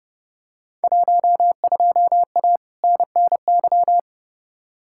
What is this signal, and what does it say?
Morse code, heard as a keyed tone, about 30 wpm, 700 Hz. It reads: 12A NNY
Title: Field Day Exchanges Encore - 30wpm